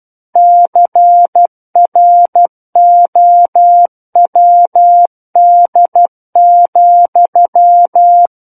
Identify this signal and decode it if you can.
Morse code, heard as a keyed tone, about 12 wpm, 700 Hz. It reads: CROWD,
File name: Book One / 03-Chapter 3 - 12 WPM